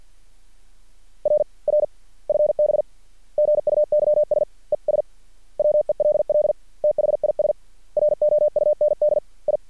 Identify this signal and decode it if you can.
Morse code, heard as a keyed tone, about 50 wpm, spaced as for 44 wpm, 600 Hz. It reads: RR FB GUYS ES WELL THIS ROUND I
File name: archived / CRSnet